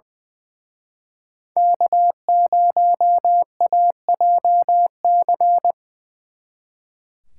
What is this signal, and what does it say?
Morse code, heard as a keyed tone, about 25 wpm, 700 Hz. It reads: K0AJC
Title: Speed Racing - USA 1x3 Call Signs 20wpm